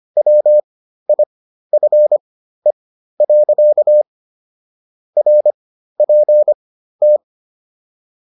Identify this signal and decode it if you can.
Morse code, heard as a keyed tone, about 25 wpm, spaced as for 13 wpm, 600 Hz. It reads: WIFE. RPT